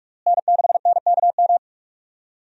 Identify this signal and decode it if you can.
Morse code, heard as a keyed tone, about 41 wpm, 700 Hz. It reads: N6NKM
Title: USA Call Signs - 5 Characters - Repeated 2x - 45wpm